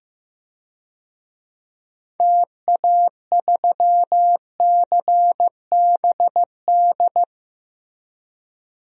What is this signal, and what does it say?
Morse code, heard as a keyed tone, about 15 wpm, 700 Hz. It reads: TA3CBD
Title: DXCC-100 Call Signs - 6 Characters - Repeated 3x - 15wpm